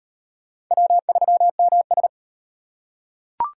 Morse code, heard as a keyed tone, about 34 wpm, 700 Hz. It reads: W3MS